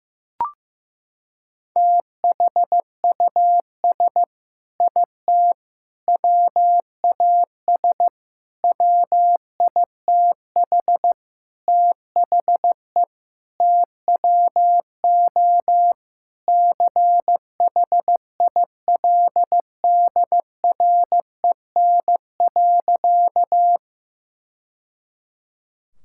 Morse code, heard as a keyed tone, about 15 wpm, 700 Hz. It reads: THUS IT WAS WITH THE TWO CHILDREN.
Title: Sentences from Top 1000 Words - Review Part 5 - 15wpm